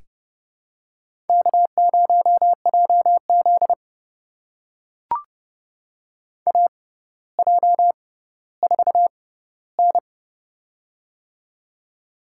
Morse code, heard as a keyed tone, about 30 wpm, spaced as for 15 wpm, 700 Hz. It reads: K0JZ AJ4N
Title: USA Call Signs - 4 Characters - 6x Character Spacing - 30wpm